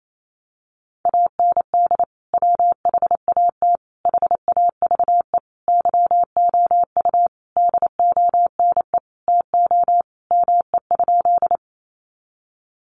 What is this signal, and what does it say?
Morse code, heard as a keyed tone, about 28 wpm, 700 Hz. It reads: AND WHAT HAVE YOU DONE TO ME?
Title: Sentences from Top 500 Words 28wpm